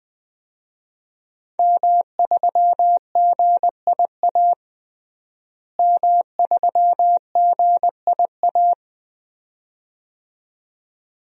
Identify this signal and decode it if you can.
Morse code, heard as a keyed tone, about 20 wpm, 700 Hz. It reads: M3GIA M3GIA
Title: DXCC-100 Call Signs - 5 Characters - Repeated 2x - 20wpm